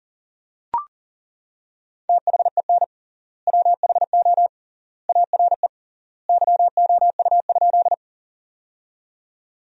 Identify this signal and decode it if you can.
Morse code, heard as a keyed tone, about 40 wpm, 700 Hz. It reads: THEN WHO ARE YOU?
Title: 3x Word-Spacing - Sentences from Top 100 Words - 40wpm